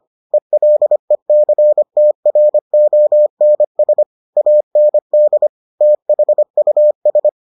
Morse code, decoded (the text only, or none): ELECTRONS AND THUS